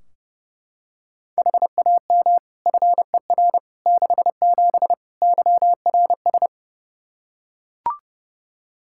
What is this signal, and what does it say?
Morse code, heard as a keyed tone, about 30 wpm, 700 Hz. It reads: HAM FER 67 YRS
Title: Practice QSO - Exchange 3 - More Info - 30wpm